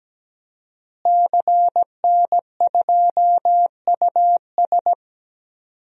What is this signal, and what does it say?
Morse code, heard as a keyed tone, about 17 wpm, 700 Hz. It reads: CN2US
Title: DXCC-100 Call Signs - 5 Characters - Repeated 2x - 17wpm